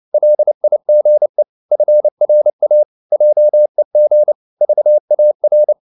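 Morse code, heard as a keyed tone, about 29 wpm, 600 Hz. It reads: LIGE FRA JEG VAR
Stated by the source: Fast (QRQ) Morse at 29 wpm, from CW med Gnister.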